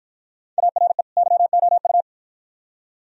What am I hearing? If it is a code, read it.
Morse code, heard as a keyed tone, about 50 wpm, 700 Hz. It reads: ARE YOU